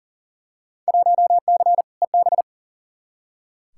Morse code, heard as a keyed tone, about 37 wpm, 700 Hz. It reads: 1C EB
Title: Field Day Exchanges Encore - 40wpm